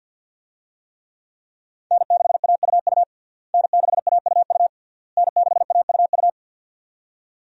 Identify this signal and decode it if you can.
Morse code, heard as a keyed tone, about 52 wpm, 700 Hz. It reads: N6AUU N6AUU N6AUU
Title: USA Call Signs - 5 Characters - Repeated 3x - 50wpm